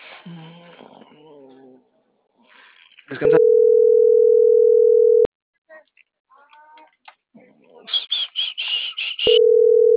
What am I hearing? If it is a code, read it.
Ringback tone (North America), 440 + 480 Hz, with speech.
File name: CENTRAL MAERSK